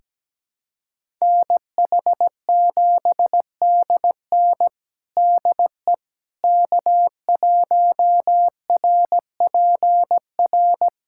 Morse code, heard as a keyed tone, about 17 wpm, 700 Hz. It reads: NH7DN DE K1RPR
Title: USA Callsign DE Callsign 17wpm